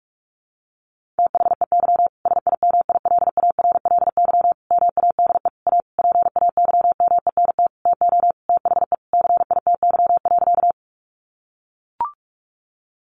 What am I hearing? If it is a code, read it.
Morse code, heard as a keyed tone, about 44 wpm, 700 Hz. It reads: THEY SIMILARLY MADE A PAYMENT TO THE CITY.
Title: Sentences from Top 1400 Words 45wpm